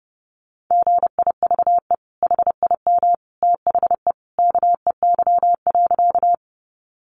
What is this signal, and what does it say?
Morse code, heard as a keyed tone, about 30 wpm, 700 Hz. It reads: GIVE HIM THE KEY.
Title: Sentences from Top 600 Words 30wpm